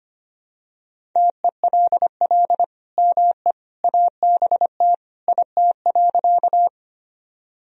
Morse code, heard as a keyed tone, about 25 wpm, 700 Hz. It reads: TELL ME ABT IT.
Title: Sentences from Top 200 Words and Abbreviations 25wpm